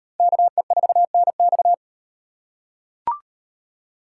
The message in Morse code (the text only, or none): KE4NX